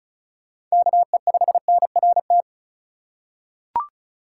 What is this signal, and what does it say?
Morse code, heard as a keyed tone, about 36 wpm, 700 Hz. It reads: KE5NRT
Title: USA Call Signs - 2x Character Spacing - 35wpm